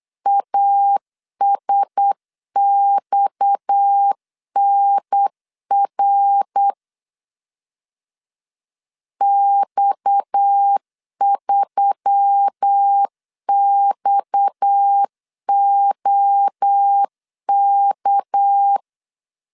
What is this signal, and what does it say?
Morse code, heard as a keyed tone, about 8 wpm, 800 Hz. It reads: ASXNR X3XOK
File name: LEZIONI / 6 PKX0 / 6 PKX 8wpm 3041